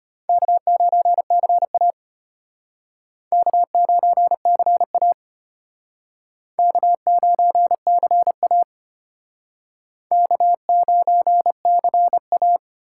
Morse code, heard as a keyed tone, about 34 wpm, 700 Hz. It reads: K9CA K9CA K9CA K9CA